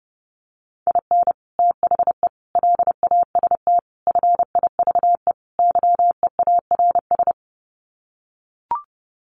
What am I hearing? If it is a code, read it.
Morse code, heard as a keyed tone, about 30 wpm, 700 Hz. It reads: IN THE LAST FIVE YEARS
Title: Sets of 5 Words Encore 30wpm